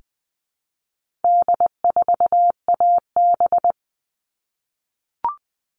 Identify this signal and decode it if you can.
Morse code, heard as a keyed tone, about 20 wpm, 700 Hz. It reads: D4AB
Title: All Cut Numbers - 4-Digit 20wpm